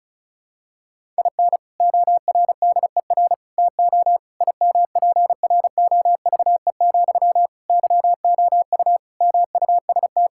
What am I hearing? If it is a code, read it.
Morse code, heard as a keyed tone, about 36 wpm, 700 Hz. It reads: IN ORDER TO IMPROVE, YOU MUST
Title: Sentences from Top 1000 Words - Review Part 5 - 35wpm